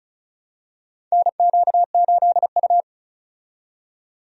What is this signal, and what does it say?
Morse code, heard as a keyed tone, about 34 wpm, 700 Hz. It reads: NQ8U
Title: USA Call Signs - 4 Characters - 35wpm